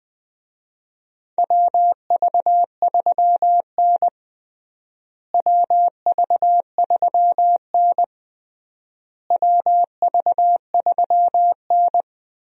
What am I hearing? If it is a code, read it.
Morse code, heard as a keyed tone, about 21 wpm, 700 Hz. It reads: WV3N WV3N WV3N
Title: USA Call Signs - 4 Characters - Repeated 3x - 20wpm